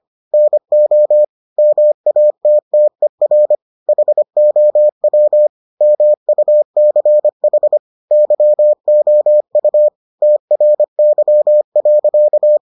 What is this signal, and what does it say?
Morse code, heard as a keyed tone, about 25 wpm, 600 Hz. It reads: NO MATTER HOW MUCH YOU TRY.